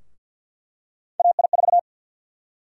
Morse code, heard as a keyed tone, about 51 wpm, 700 Hz. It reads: AI4
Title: USA Call Sign Prefixes - Repeated 2x - 50wpm